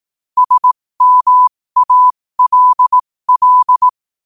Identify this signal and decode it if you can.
Morse code, heard as a keyed tone, about 18 wpm, spaced as for 15 wpm, 1000 Hz. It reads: SMALL
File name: qotdfiles / 15wpm